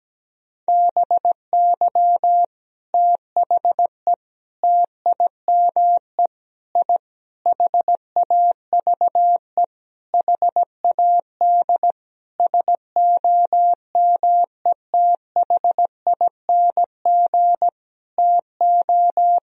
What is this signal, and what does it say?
Morse code, heard as a keyed tone, about 17 wpm, 700 Hz. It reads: BY THE TIME I HAVE HAD SOMETHING TO
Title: Sentences from Top 1000 Words - Review Part 1 - 17wpm